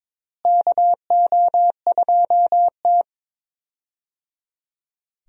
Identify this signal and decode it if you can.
Morse code, heard as a keyed tone, about 22 wpm, 700 Hz. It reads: KO2T
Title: USA Call Signs - 4 Characters - Repeated 3x - 22wpm